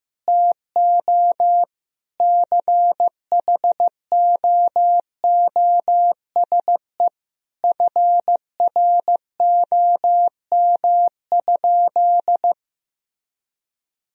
Morse code, heard as a keyed tone, about 15 wpm, 700 Hz. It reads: TO CHOOSE FROM?
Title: Sentences from Top 1000 Words - Review Part 4 - 15wpm